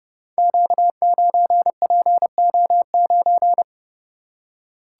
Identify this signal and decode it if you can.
Morse code, heard as a keyed tone, about 30 wpm, 700 Hz. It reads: Q9PO9
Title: Code Groups - 5 Letter:Numbers - 30wpm